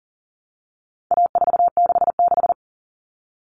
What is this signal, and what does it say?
Morse code, heard as a keyed tone, about 40 wpm, 700 Hz. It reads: A466
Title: Common Cut Numbers - 4-Digit 40wpm